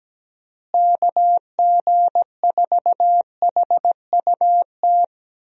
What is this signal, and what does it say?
Morse code, heard as a keyed tone, about 17 wpm, 700 Hz. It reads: KG4HUT
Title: USA Call Signs - 6 Characters - Repeated 3x - 17wpm